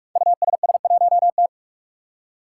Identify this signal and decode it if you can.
Morse code, heard as a keyed tone, about 44 wpm, 700 Hz. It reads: USS1T
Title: Five Challenging Letter-Number Combinations 45wpm